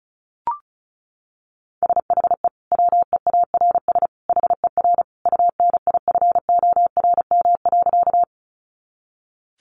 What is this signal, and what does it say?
Morse code, heard as a keyed tone, about 34 wpm, 700 Hz. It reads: SHE WEARS HER UNIFORM.
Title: Easy Phrases 35wpm